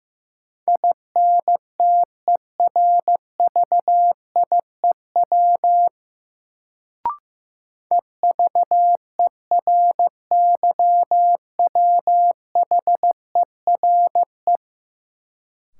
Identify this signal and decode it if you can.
Morse code, heard as a keyed tone, about 15 wpm, 700 Hz. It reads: INTERVIEW EVERYWHERE
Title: Sentences from Top 1600 Words - New Word Drill Alt - 15wpm